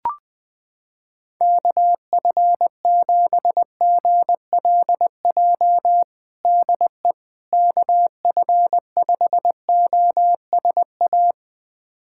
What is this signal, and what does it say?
Morse code, heard as a keyed tone, about 20 wpm, 700 Hz. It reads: KF7GLJ DE KF5OSA
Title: USA Callsign DE Callsign - 6 Characters - 20wpm